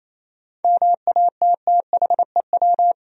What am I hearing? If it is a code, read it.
Morse code, heard as a keyed tone, about 28 wpm, 700 Hz. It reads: MATTHEW
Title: USA Common Names - 28wpm